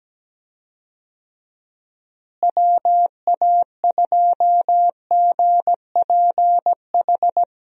Morse code, heard as a keyed tone, about 17 wpm, 700 Hz. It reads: WA2GPH
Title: USA Call Signs - 6 Characters - Repeated 3x - 17wpm